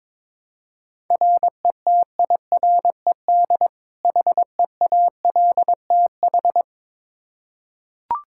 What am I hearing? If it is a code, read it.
Morse code, heard as a keyed tone, about 21 wpm, 700 Hz. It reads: RETIRED HEALTH
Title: Practice QSO - Exchange 3 - More Info - 22wpm